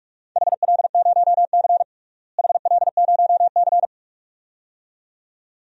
Morse code, heard as a keyed tone, about 42 wpm, 700 Hz. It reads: HL0C HL0C
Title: DXCC-100 Call Signs - 4 Characters - Repeated 2x - 45wpm